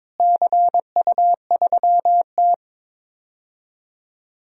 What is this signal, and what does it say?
Morse code, heard as a keyed tone, about 22 wpm, 700 Hz. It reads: CU3T
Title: DXCC-100 Call Signs - 4 Characters - Repeated 2x - 22wpm